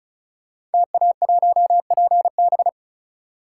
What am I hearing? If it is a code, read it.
Morse code, heard as a keyed tone, about 36 wpm, 700 Hz. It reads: TA1PB
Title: Code Groups - 5 Letter:Numbers - 35wpm